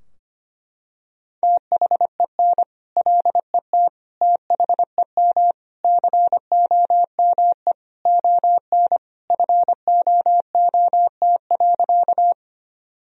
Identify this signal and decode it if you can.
Morse code, heard as a keyed tone, about 25 wpm, 700 Hz. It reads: THEN LET THEM COME ON FOOT.
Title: Sentences from Top 1000 Words - Review Part 3 - 25wpm